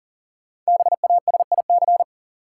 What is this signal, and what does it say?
Morse code, heard as a keyed tone, about 40 wpm, 700 Hz. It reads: BASIC
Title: Sentences from Top 1000 Words - Review Part 3 - New Word Drill - 40wpm